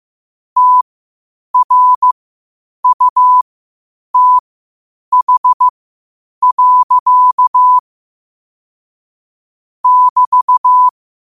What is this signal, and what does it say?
Morse code, heard as a keyed tone, about 15 wpm, spaced as for 8 wpm, 1000 Hz. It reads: TRUTH. <BT>
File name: qotdfiles / 8wpm